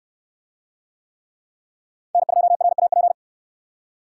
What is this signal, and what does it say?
Morse code, heard as a keyed tone, about 64 wpm, 700 Hz. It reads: N2RAP